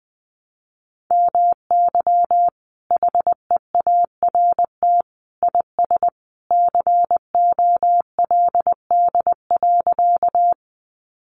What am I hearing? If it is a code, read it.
Morse code, heard as a keyed tone, about 20 wpm, 700 Hz. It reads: MY HEART IS COLD.
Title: Sentences from Top 1200 Words 20wpm